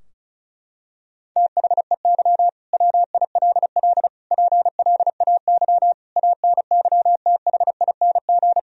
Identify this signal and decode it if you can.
Morse code, heard as a keyed tone, about 36 wpm, 700 Hz. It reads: THEY WILL PLAY ANYTHING
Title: Sentences from Top 1000 Words - Review Part 4 - 35wpm